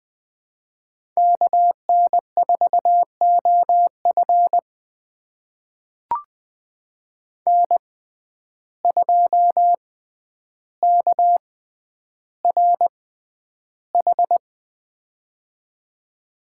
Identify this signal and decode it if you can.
Morse code, heard as a keyed tone, about 20 wpm, spaced as for 10 wpm, 700 Hz. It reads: KN4OF N2KRH
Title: USA Call Signs - 5 Characters - 6x Character Spacing - 20wpm